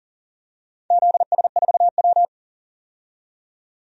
Morse code, heard as a keyed tone, about 37 wpm, 700 Hz. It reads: ZS4W
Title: DXCC-100 Call Signs - 4 Characters - 40wpm